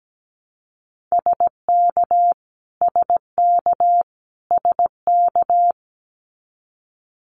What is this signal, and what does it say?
Morse code, heard as a keyed tone, about 17 wpm, 700 Hz. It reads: SK SK SK
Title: Canadian Provinces and Territories - Repeated 3x - 17wpm